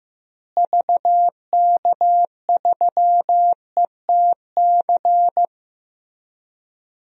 Morse code, heard as a keyed tone, about 15 wpm, 700 Hz. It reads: VK3ETC
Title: DXCC-100 Call Signs - 6 Characters - Repeated 3x - 15wpm